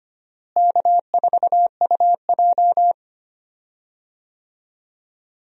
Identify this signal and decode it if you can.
Morse code, heard as a keyed tone, about 25 wpm, 700 Hz. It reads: K4UJ
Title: USA Call Signs - 4 Characters - Repeated 3x - 25wpm